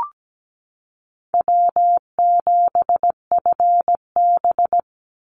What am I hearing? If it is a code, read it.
Morse code, heard as a keyed tone, about 17 wpm, 700 Hz. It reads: W7FB
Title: Call Signs - 17wpm